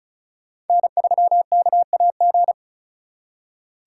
Morse code, heard as a keyed tone, about 44 wpm, 700 Hz. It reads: N3KAG